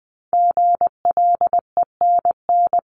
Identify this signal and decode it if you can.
Morse code, heard as a keyed tone, about 20 wpm, 700 Hz. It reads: GLENN
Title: CWT Exchange 20wpm